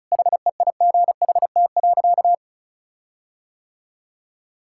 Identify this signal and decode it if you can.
Morse code, heard as a keyed tone, about 36 wpm, 700 Hz. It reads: HEIGHT.
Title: Sentences from Top 1600 Words 35wpm